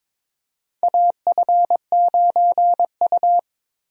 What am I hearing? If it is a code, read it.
Morse code, heard as a keyed tone, about 21 wpm, 700 Hz. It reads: AF9U
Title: USA Call Signs - 4 Characters - Repeated 2x - 22wpm